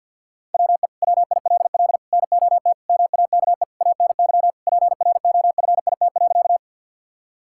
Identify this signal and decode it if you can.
Morse code, heard as a keyed tone, about 50 wpm, 700 Hz. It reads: WE WILL NOT MAKE ANY PROFIT.
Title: Sentences from Top 1000 Words - Review Part 5 - 50wpm